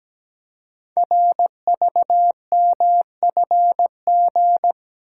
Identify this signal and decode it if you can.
Morse code, heard as a keyed tone, about 17 wpm, 700 Hz. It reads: RVMFG
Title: Code Groups - 5 Letter:Numbers - 17wpm